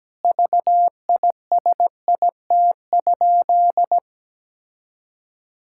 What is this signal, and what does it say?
Morse code, heard as a keyed tone, about 17 wpm, 700 Hz. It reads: VISIT?
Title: Sentences from Top 1000 Words - Review Part 6 - 17wpm